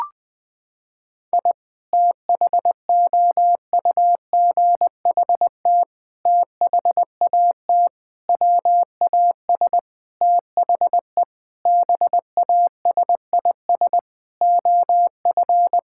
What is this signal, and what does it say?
Morse code, heard as a keyed tone, about 21 wpm, 700 Hz. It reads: I THOUGHT THAT WAS THE BASIS OF
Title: Sentences from Top 1000 Words - Review Part 5 - 20wpm